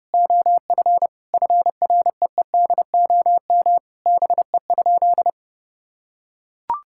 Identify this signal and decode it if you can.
Morse code, heard as a keyed tone, about 32 wpm, 700 Hz. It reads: OF FREEDOM BE?
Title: Sentences from Top 1000 Words - Review Part 5 - 30wpm